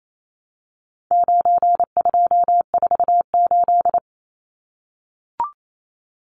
Morse code, heard as a keyed tone, about 28 wpm, 700 Hz. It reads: 9248
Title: Numbers - 4-Digit 28wpm